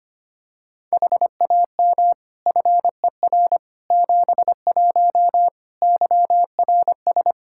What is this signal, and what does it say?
Morse code, heard as a keyed tone, about 25 wpm, 700 Hz. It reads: HAM FER 71 YRS
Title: K1USN SST QSO Practice 25wpm